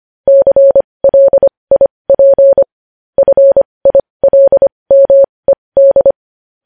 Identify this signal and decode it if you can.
Morse code, heard as a keyed tone, about 25 wpm, spaced as for 20 wpm, 550 Hz. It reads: CLIP FILMED
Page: news-headlines-in-morse-code-at-20-wpm